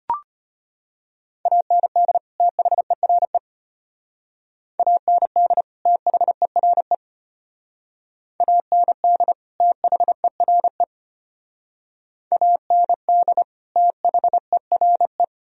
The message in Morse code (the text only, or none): AND THERE AND THERE AND THERE AND THERE